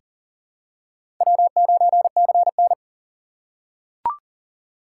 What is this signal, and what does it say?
Morse code, heard as a keyed tone, about 39 wpm, 700 Hz. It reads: W9CN
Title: USA Call Signs - 4 Characters - 6x Character Spacing - 40wpm